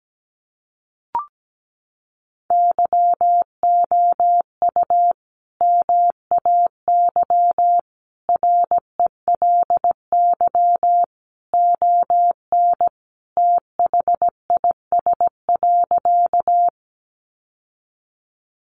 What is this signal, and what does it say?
Morse code, heard as a keyed tone, about 17 wpm, 700 Hz. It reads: YOU MAY RELY ON THIS.
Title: Sentences from Top 1200 Words 17wpm